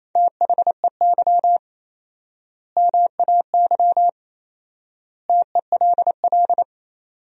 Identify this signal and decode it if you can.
Morse code, heard as a keyed tone, about 28 wpm, 700 Hz. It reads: THEY MAY TELL
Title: 4x Word-Spacing - Sentences from Top 300 Words - 28wpm